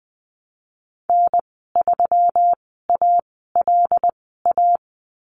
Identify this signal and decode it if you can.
Morse code, heard as a keyed tone, about 20 wpm, spaced as for 15 wpm, 700 Hz. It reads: N3ALA
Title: 122 Lesson 20g - Number 3 - Call Signs 20wpm